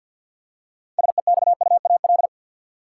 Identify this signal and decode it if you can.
Morse code, heard as a keyed tone, about 52 wpm, 700 Hz. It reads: SEXUAL
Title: Six Letter Words - 50wpm